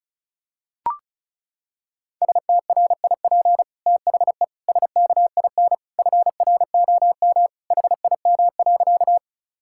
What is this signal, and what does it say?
Morse code, heard as a keyed tone, about 34 wpm, 700 Hz. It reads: STRIP THE SKIN FROM HIM.
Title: Sentences from Top 1600 Words 35wpm